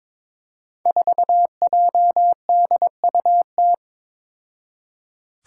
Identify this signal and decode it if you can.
Morse code, heard as a keyed tone, about 22 wpm, 700 Hz. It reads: 4JDUT
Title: Five Challenging Letter-Number Combinations 22wpm